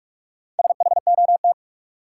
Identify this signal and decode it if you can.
Morse code, heard as a keyed tone, about 42 wpm, 700 Hz. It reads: SHOT
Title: Sentences from Top 1100 Words - New Word Drill Alt - 45wpm